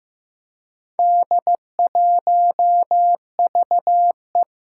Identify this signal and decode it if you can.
Morse code, heard as a keyed tone, about 15 wpm, 700 Hz. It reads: D1VE
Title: Four Challenging Letter-Number Combinations 15wpm